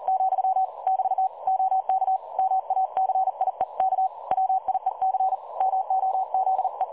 Morse code, heard as a keyed tone, about 40 wpm, 760 Hz. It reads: , <BT> OK MACIEK WIEC MOZE
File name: dlazaawansowanych / nagraniazpasma / websdr recording 3565.0kHz